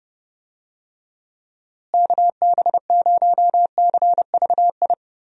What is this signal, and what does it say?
Morse code, heard as a keyed tone, about 30 wpm, 700 Hz. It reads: KB0CVI
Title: USA Call Signs - 6 Characters - Repeated 2x - 30wpm